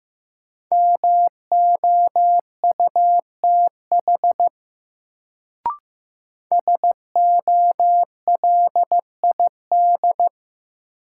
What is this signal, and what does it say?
Morse code, heard as a keyed tone, about 15 wpm, 700 Hz. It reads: MOUTH SOLID
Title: Sentences from Top 1500 Words - New Word Drill - 15wpm